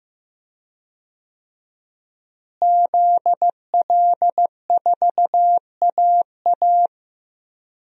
Morse code, heard as a keyed tone, about 15 wpm, 700 Hz. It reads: ZL4AA
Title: DXCC-100 Call Signs - 5 Characters - Repeated 3x - 15wpm